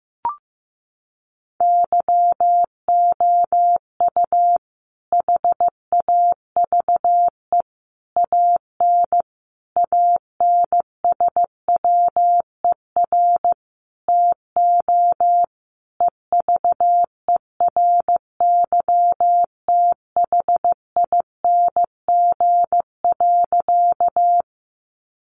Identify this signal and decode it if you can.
Morse code, heard as a keyed tone, about 15 wpm, 700 Hz. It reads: YOU HAVE AN ANSWER TO EVERYTHING.
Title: Sentences from Top 700 Words 15wpm